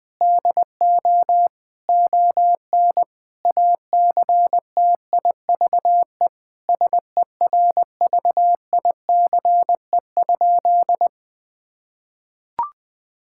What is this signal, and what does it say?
Morse code, heard as a keyed tone, about 21 wpm, 700 Hz. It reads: DO ON ACTIVE SERVICE?
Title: Sentences from Top 1000 Words - Review Part 6 - 20wpm